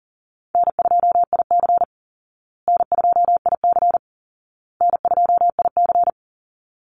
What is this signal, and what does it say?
Morse code, heard as a keyed tone, about 40 wpm, 700 Hz. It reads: N2IC N2IC N2IC
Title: Call Signs Repeated 2x - 40wpm